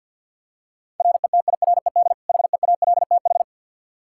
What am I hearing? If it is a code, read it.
Morse code, heard as a keyed tone, about 51 wpm, 700 Hz. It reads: RETIRED HEALTH
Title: Practice QSO - Exchange 3 - More Info - 50wpm